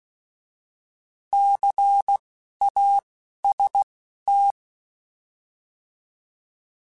Morse code, heard as a keyed tone, about 16 wpm, spaced as for 12 wpm, 800 Hz. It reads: CAST
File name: Alphabetical / Alphabetical mp3s / Cs @10